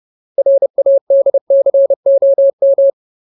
Morse code, heard as a keyed tone, about 30 wpm, 550 Hz. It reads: RADCOM
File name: Ultimate Word List / 30WPM Ultimate